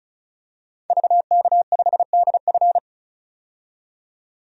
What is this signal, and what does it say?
Morse code, heard as a keyed tone, about 36 wpm, 700 Hz. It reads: VK5DF
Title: DXCC-100 Call Signs - 5 Characters - 35wpm